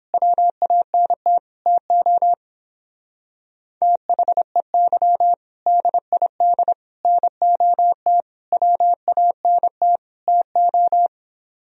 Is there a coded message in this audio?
Yes: Morse code from 0.1 to 11.1 s, heard as a keyed tone, about 29 wpm, 700 Hz. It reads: WANT TO THEY DID NOT WANT TO